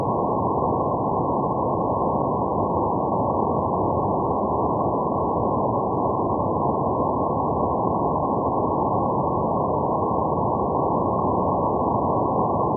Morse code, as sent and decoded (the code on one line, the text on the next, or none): none